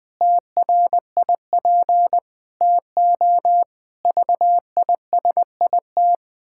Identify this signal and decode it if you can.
Morse code, heard as a keyed tone, about 20 wpm, 700 Hz. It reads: TRIP TO VISIT
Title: Pangram Challenge v2 - 20wpm